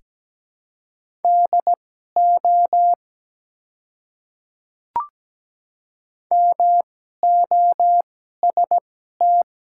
Morse code, heard as a keyed tone, about 17 wpm, spaced as for 12 wpm, 700 Hz. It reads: DO MOST
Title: 2x Character Spacing - Top 200 Words - 17wpm